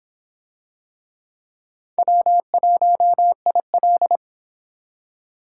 Morse code, heard as a keyed tone, about 26 wpm, 700 Hz. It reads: W1IL